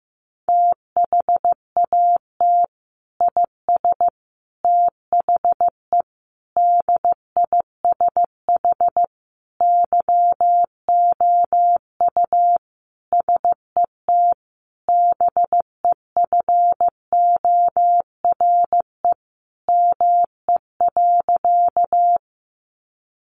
Morse code, heard as a keyed tone, about 15 wpm, 700 Hz. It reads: THAT IS THE DISH YOU SET BEFORE ME.
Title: Sentences from Top 1300 Words 15wpm